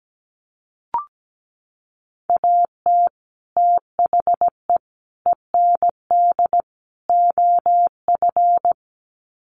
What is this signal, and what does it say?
Morse code, heard as a keyed tone, about 17 wpm, 700 Hz. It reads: AT THE END OF
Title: Sets of 4 Words 17wpm